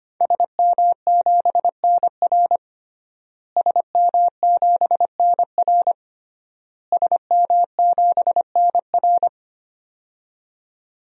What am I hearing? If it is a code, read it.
Morse code, heard as a keyed tone, about 25 wpm, 700 Hz. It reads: SM7NR SM7NR SM7NR
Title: DXCC-100 Call Signs - 5 Characters - Repeated 3x - 25wpm